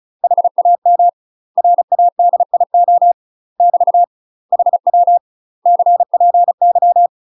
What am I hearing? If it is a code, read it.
Morse code, heard as a keyed tone, about 34 wpm, 700 Hz. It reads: HAM RADIO <BT> HW CPY
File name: UK MASTER QSOs TRAINING / 35WPM / UK MASTER QSOs 0034